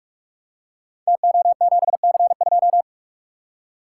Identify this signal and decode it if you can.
Morse code, heard as a keyed tone, about 44 wpm, 700 Hz. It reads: TO7C2
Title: Code Groups - 5 Letter:Numbers - 45wpm